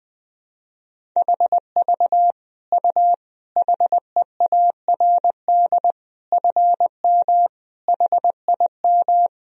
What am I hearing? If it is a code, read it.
Morse code, heard as a keyed tone, about 20 wpm, 700 Hz. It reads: HV U HEARD FM HIM
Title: Sentences from Top 400 Words and Abbreviations 20wpm